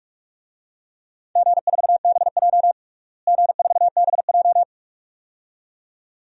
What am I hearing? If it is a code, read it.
Morse code, heard as a keyed tone, about 46 wpm, 700 Hz. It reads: G4BJ G4BJ
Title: DXCC-100 Call Signs - 4 Characters - Repeated 2x - 45wpm